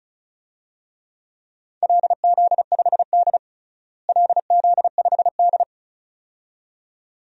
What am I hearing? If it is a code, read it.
Morse code, heard as a keyed tone, about 34 wpm, 700 Hz. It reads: LZ5D LZ5D
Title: DXCC-100 Call Signs - 4 Characters - Repeated 2x - 35wpm